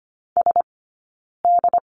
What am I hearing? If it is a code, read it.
Morse code, heard as a keyed tone, about 25 wpm, spaced as for 9 wpm, 690 Hz. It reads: SD
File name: Lesson 15d - Number 5 - Two Characters 25wpm